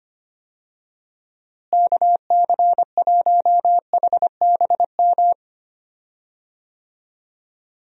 Morse code, heard as a keyed tone, about 25 wpm, 700 Hz. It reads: KC1HBM
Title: USA Call Signs - 6 Characters - Repeated 3x - 25wpm